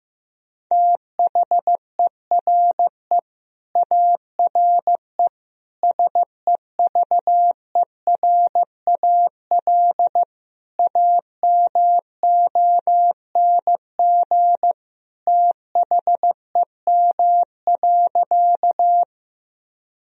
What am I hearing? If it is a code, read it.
Morse code, heard as a keyed tone, about 16 wpm, 700 Hz. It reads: THERE ARE SEVERAL AMONG THEM.
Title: Sentences from Top 1000 Words - Review Part 2 - 15wpm